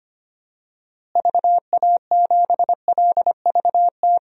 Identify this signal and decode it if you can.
Morse code, heard as a keyed tone, about 25 wpm, 700 Hz. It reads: VA7LVT
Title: DXCC-100 Call Signs - 6 Characters - Repeated 3x - 25wpm